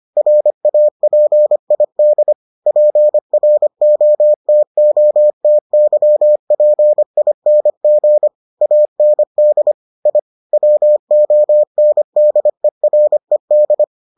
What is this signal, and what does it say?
Morse code, heard as a keyed tone, about 25 wpm, 600 Hz. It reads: RAPID PROTOTYPING AND I WONDERED